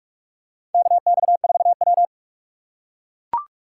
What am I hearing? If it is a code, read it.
Morse code, heard as a keyed tone, about 44 wpm, 700 Hz. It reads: KX4W
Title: USA Call Signs - 3x Character Spacing - 45wpm